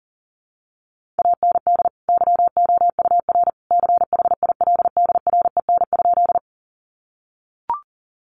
Morse code, heard as a keyed tone, about 40 wpm, 700 Hz. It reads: AND YOUR CHILDREN?
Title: Sentences from Top 300 Words 40wpm